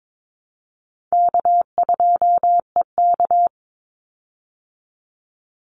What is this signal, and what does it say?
Morse code, heard as a keyed tone, about 22 wpm, 700 Hz. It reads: K2EK
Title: Easy Call Signs Repeated 2x - 22wpm